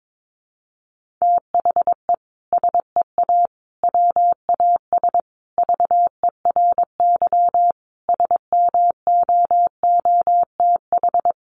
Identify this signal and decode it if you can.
Morse code, heard as a keyed tone, about 22 wpm, 700 Hz. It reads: THE SEA WAS VERY SMOOTH
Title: Sentences from Top 1400 Words 22wpm